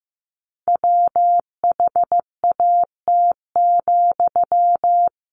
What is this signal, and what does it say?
Morse code, heard as a keyed tone, about 15 wpm, 700 Hz. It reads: WHAT,
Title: Sentences from Top 1100 Words 15wpm